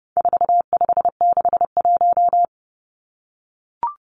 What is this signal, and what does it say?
Morse code, heard as a keyed tone, about 30 wpm, 700 Hz. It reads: 4561
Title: Numbers - 4-Digit 30wpm